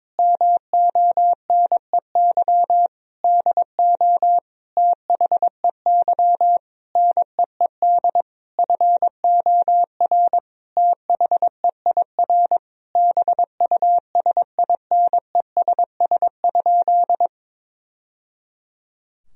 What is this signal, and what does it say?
Morse code, heard as a keyed tone, about 22 wpm, 700 Hz. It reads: MONEY DO THEY NEED FOR THEIR BUSINESS?
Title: Sentences from Top 1000 Words - Review Part 4 - 22wpm